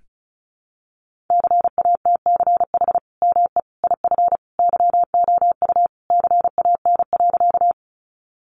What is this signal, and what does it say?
Morse code, heard as a keyed tone, about 34 wpm, 700 Hz. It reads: CATCH ME IF YOU CAN.
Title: Sentences from Top 1500 Words 35wpm